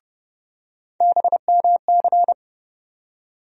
Morse code, heard as a keyed tone, about 30 wpm, 700 Hz. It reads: BMC
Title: USA Call Sign Postfixes - 3 Letters 30wpm